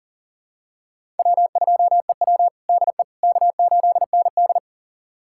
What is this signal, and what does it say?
Morse code, heard as a keyed tone, about 37 wpm, 700 Hz. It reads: W2EW DE K8ND
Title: USA Callsign DE Callsign - 4 Characters - 40wpm